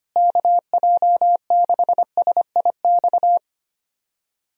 Morse code, heard as a keyed tone, about 25 wpm, 700 Hz. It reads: KJ6SIX
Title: USA Call Signs - 6 Characters - Repeated 2x - 25wpm